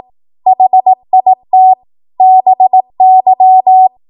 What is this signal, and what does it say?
Morse code, heard as a keyed tone, about 18 wpm, 750 Hz. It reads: HIT BY
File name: Morse / 18WPM